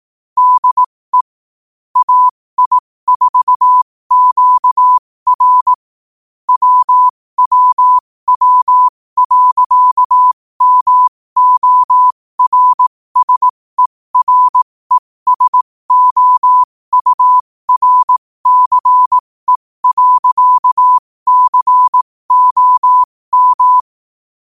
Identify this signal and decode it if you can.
Morse code, heard as a keyed tone, about 18 wpm, spaced as for 15 wpm, 1000 Hz. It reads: DE AI4QR WWW.MORSERESOURCE.COM
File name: qotdfiles / 15wpm